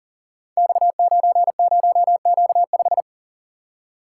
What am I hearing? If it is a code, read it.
Morse code, heard as a keyed tone, about 41 wpm, 700 Hz. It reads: X90Q5
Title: Code Groups - 5 Letter:Numbers - 40wpm